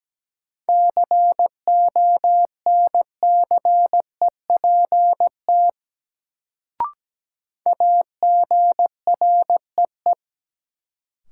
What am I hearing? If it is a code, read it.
Morse code, heard as a keyed tone, about 17 wpm, 700 Hz. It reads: CONCEPT AGREE
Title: Sentences from Top 1100 Words - New Word Drill Alt - 17wpm